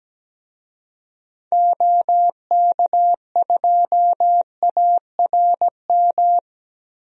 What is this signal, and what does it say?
Morse code, heard as a keyed tone, about 17 wpm, 700 Hz. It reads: OK2ARM
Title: DXCC-100 Call Signs - 6 Characters - Repeated 3x - 17wpm